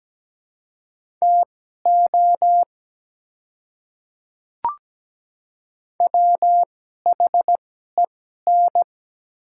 Morse code, heard as a keyed tone, about 17 wpm, spaced as for 12 wpm, 700 Hz. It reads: TO WHEN
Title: 2x Character Spacing - Top 100 Words - 17wpm